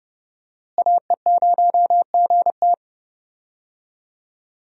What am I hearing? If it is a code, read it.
Morse code, heard as a keyed tone, about 30 wpm, 700 Hz. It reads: AE0GT
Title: USA Call Signs - 5 Characters - Repeated 2x - 30wpm